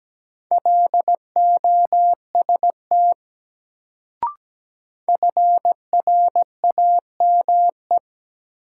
Morse code, heard as a keyed tone, about 17 wpm, 700 Hz. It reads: LOST FRAME
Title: Sentences from Top 1400 Words - New Word Drill Alt - 17wpm